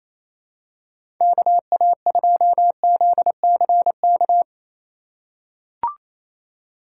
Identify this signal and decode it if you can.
Morse code, heard as a keyed tone, about 28 wpm, 700 Hz. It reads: KA2ZCK
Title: USA Call Signs - 6 Characters - 4x Character Spacing - 28wpm